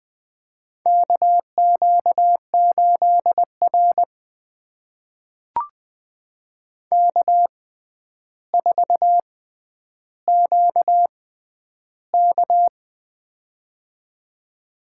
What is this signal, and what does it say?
Morse code, heard as a keyed tone, about 20 wpm, spaced as for 10 wpm, 700 Hz. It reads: KQ8R K4QK
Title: USA Call Signs - 6x Character Spacing - 20wpm